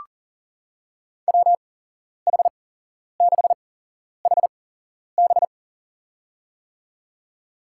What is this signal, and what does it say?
Morse code, heard as a keyed tone, about 40 wpm, spaced as for 11 wpm, 700 Hz. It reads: WH6HB
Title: USA Call Signs - 5 Characters - 8x Character Spacing - 40wpm